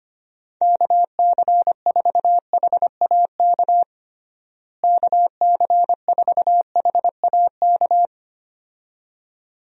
Morse code, heard as a keyed tone, about 25 wpm, 700 Hz. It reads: KC4HAK KC4HAK
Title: USA Call Signs - 6 Characters - Repeated 2x - 25wpm